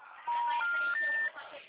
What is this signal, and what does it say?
Special information tone, with speech.